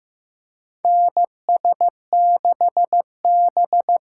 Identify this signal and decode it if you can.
Morse code, heard as a keyed tone, about 15 wpm, 700 Hz. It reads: NS6B
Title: USA Call Signs - 4 Characters - 6x Character Spacing - 15wpm